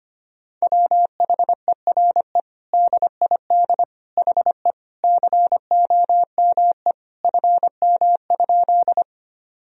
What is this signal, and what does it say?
Morse code, heard as a keyed tone, about 25 wpm, 700 Hz. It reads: WHERE DID HE COME FM?
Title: Sentences from Top 200 Words and Abbreviations 25wpm